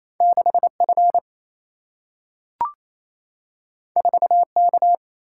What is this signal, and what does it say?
Morse code, heard as a keyed tone, about 28 wpm, 700 Hz. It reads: 6F 4K
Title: Single-Letter Single-Number 28wpm